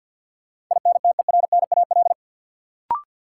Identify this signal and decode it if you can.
Morse code, heard as a keyed tone, about 48 wpm, 700 Hz. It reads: INTERNAL
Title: Sentences from Top 1400 Words - New Word Drill - 50wpm